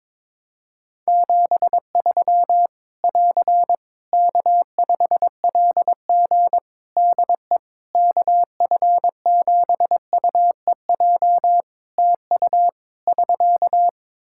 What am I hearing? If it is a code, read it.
Morse code, heard as a keyed tone, about 22 wpm, 700 Hz. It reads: 73 <AR> K5LG DE KF7UEJ TU <SK>
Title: K1USN SST QSO Practice 22wpm